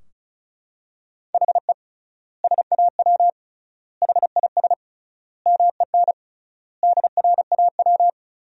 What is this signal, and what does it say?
Morse code, heard as a keyed tone, about 36 wpm, 700 Hz. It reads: HE SAW HIS MEN DRAW
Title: 3x Word-Spacing - Sentences from Top 300 Words - 35wpm